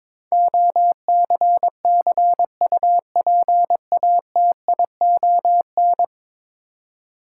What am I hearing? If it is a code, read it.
Morse code, heard as a keyed tone, about 22 wpm, 700 Hz. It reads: OCCUPATION
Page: morse-code-ninja-practice-22wpm